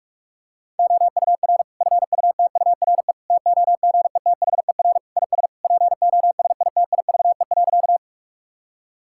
Morse code, heard as a keyed tone, about 41 wpm, 700 Hz. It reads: OUR FUTURE TOGETHER IS POSITIVE.
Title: Sentences from Top 1000 Words - Review Part 5 - 45wpm